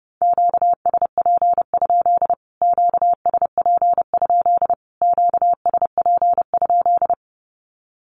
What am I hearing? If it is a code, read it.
Morse code, heard as a keyed tone, about 30 wpm, 700 Hz. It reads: QSP? QSP? QSP?
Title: Q-Codes Repeated 3x 30wpm